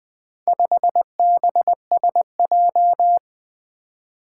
Morse code, heard as a keyed tone, about 23 wpm, 700 Hz. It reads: 5BSJ
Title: Speed-Racing - Four Letter-Number Combinations - 20wpm